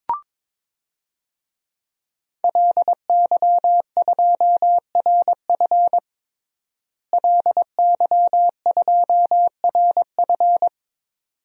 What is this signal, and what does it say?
Morse code, heard as a keyed tone, about 22 wpm, 700 Hz. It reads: LY2RF LY2RF
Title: DXCC-100 Call Signs - 5 Characters - Repeated 2x - 22wpm